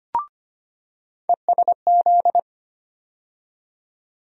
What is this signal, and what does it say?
Morse code, heard as a keyed tone, about 25 wpm, 700 Hz. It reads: ESZ
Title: USA Call Sign Postfixes - 3 Letters 25wpm